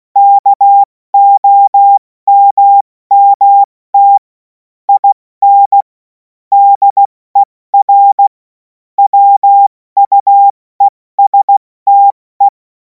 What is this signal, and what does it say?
Morse code, heard as a keyed tone, about 16 wpm, 800 Hz. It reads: KOMMT IN DER WUESTE
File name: Morsekurs / Witze / Witz 03. März 2026 70 F80